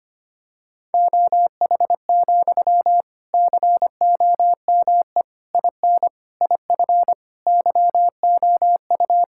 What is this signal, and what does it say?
Morse code, heard as a keyed tone, about 25 wpm, 700 Hz. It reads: OH, COME IN IF YOU
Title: Sentences from Top 1600 Words 25wpm